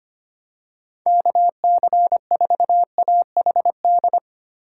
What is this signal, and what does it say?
Morse code, heard as a keyed tone, about 25 wpm, 700 Hz. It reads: KC4AHD
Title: USA Call Signs - 6 Characters - 3x Character Spacing - 25wpm